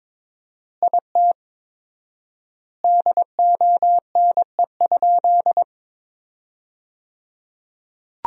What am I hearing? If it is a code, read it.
Morse code, heard as a keyed tone, about 22 wpm, 700 Hz. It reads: IT DONE?
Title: 4x Word-Spacing - Sentences from Top 500 Words - 22wpm